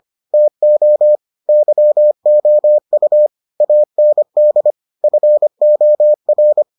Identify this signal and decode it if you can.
Morse code, heard as a keyed tone, about 25 wpm, 600 Hz. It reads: TO YOU AND FOR